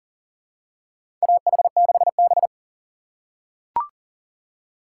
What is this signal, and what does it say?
Morse code, heard as a keyed tone, about 40 wpm, 700 Hz. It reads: AH6B
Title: USA Call Signs - 6x Character Spacing - 40wpm